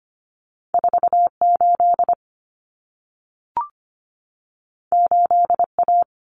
Morse code, heard as a keyed tone, about 25 wpm, 700 Hz. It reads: 48 8A
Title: Common Cut Numbers - 2-Digit 25wpm